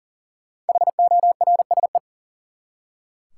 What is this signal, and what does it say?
Morse code, heard as a keyed tone, about 37 wpm, 700 Hz. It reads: HORSE
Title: Five Letter Words - 40wpm